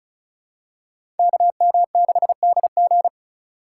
Morse code, heard as a keyed tone, about 34 wpm, 700 Hz. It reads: KM6DG
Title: USA Call Signs - 4x Character Spacing - 35wpm